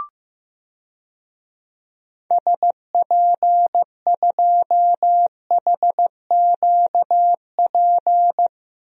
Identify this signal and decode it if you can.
Morse code, heard as a keyed tone, about 15 wpm, 700 Hz. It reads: SP2HQP
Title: DXCC-100 Call Signs - 6 Characters - Repeated 3x - 15wpm